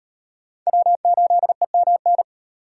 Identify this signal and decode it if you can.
Morse code, heard as a keyed tone, about 34 wpm, 700 Hz. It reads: W8EMN